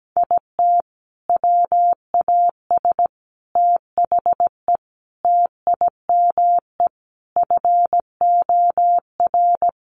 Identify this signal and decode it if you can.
Morse code, heard as a keyed tone, about 17 wpm, 700 Hz. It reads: IT WAS THE TIME FOR
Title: Sentences from Top 700 Words 17wpm